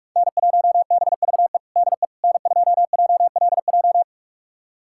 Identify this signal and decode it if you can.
Morse code, heard as a keyed tone, about 44 wpm, 700 Hz. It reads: N1BVE DE N2JLJ
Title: USA Callsign DE Callsign - 5 Characters - 45wpm